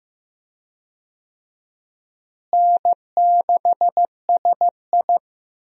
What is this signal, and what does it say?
Morse code, heard as a keyed tone, about 15 wpm, 700 Hz. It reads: N6SI
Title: USA Call Signs - 4 Characters - Repeated 3x - 15wpm